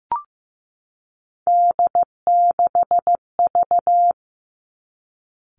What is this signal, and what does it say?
Morse code, heard as a keyed tone, about 15 wpm, 700 Hz. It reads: D6V
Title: Three Challenging Letter-Number Combinations 15wpm